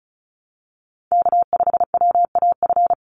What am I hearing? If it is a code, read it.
Morse code, heard as a keyed tone, about 34 wpm, 700 Hz. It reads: K5WAF
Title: Call Signs Repeated 2x - 35wpm